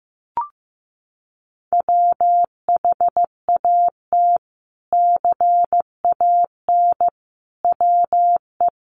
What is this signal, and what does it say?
Morse code, heard as a keyed tone, about 15 wpm, 700 Hz. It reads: WHAT CAN WE
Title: Sentences from Top 1100 Words 15wpm